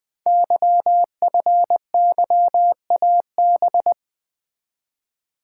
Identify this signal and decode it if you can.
Morse code, heard as a keyed tone, about 20 wpm, 700 Hz. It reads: YFYAB
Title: Code Groups - 5 Letter:Numbers - 20wpm